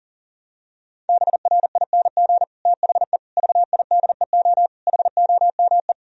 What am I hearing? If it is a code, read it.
Morse code, heard as a keyed tone, about 40 wpm, 700 Hz. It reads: BRING THE VIDEO HOME
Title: Sentences from Top 1000 Words - Review Part 4 - 40wpm